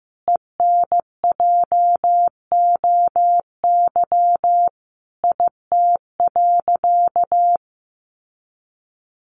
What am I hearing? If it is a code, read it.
Morse code, heard as a keyed tone, about 15 wpm, 700 Hz. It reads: ENJOY IT.
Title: Sentences from Top 1500 Words 15wpm